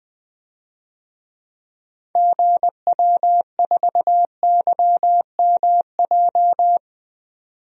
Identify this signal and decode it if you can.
Morse code, heard as a keyed tone, about 20 wpm, 700 Hz. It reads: GW4YMJ
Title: DXCC-100 Call Signs - 6 Characters - 20wpm